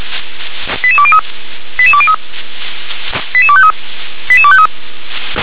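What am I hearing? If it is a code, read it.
Five-tone selective call (ZVEI), heard as quick runs of five tones, tones of about 70 ms. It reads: 88203 88203 88253 88253
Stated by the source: ZVEI five-tone selective call received on SDR.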